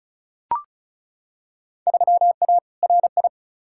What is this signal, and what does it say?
Morse code, heard as a keyed tone, about 36 wpm, 700 Hz. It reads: 3A RI
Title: Field Day Exchanges - 35wpm